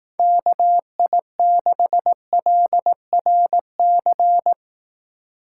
Morse code, heard as a keyed tone, about 18 wpm, 700 Hz. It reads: KI6LRC
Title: Speed Racing - USA Call Signs 15wpm